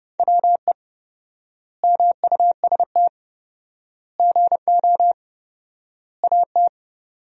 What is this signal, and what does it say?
Morse code, heard as a keyed tone, about 30 wpm, 700 Hz. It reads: WE MUST GO AT
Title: 4x Word-Spacing - Sentences from Top 300 Words - 30wpm